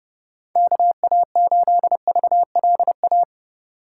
Morse code, heard as a keyed tone, about 30 wpm, 700 Hz. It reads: KA8VLA
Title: USA Call Signs - 6 Characters - 3x Character Spacing - 30wpm